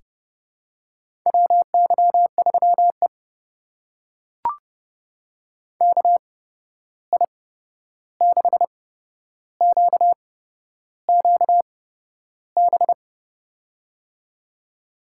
Morse code, heard as a keyed tone, about 30 wpm, spaced as for 11 wpm, 700 Hz. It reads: WY3E KI6QQB
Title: USA Call Signs - 8x Character Spacing - 30wpm